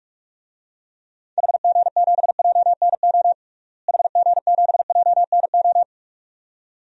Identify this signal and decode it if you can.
Morse code, heard as a keyed tone, about 46 wpm, 700 Hz. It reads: HG7JNO HG7JNO
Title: DXCC-100 Call Signs - 6 Characters - Repeated 2x - 45wpm